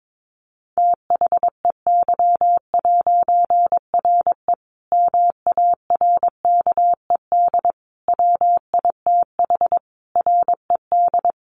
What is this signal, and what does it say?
Morse code, heard as a keyed tone, about 22 wpm, 700 Hz. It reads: THEY'RE MARKED WITH RED
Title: A Sound Of Thunder 22wpm